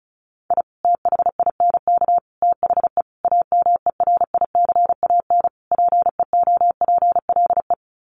Morse code, heard as a keyed tone, about 34 wpm, 700 Hz. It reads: I THINK THE AMERICAN PEOPLE
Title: Sets of 5 Words Encore 35wpm